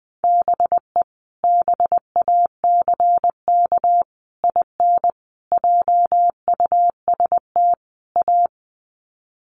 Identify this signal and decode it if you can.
Morse code, heard as a keyed tone, about 19 wpm, 700 Hz. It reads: BE BACK IN JUST A
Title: Sets of 5 Words Encore 20wpm